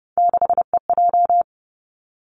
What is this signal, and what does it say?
Morse code, heard as a keyed tone, about 30 wpm, 700 Hz. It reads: 6EJ
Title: Three Challenging Letter-Number Combinations 30wpm